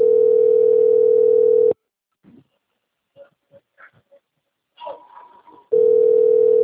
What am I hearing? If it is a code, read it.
Ringback tone (North America), 440 + 480 Hz.
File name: out- - - - - .13843